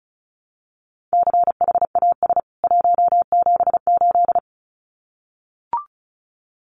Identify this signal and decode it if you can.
Morse code, heard as a keyed tone, about 34 wpm, 700 Hz. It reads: CHAS 178
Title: CWT Exchange 35wpm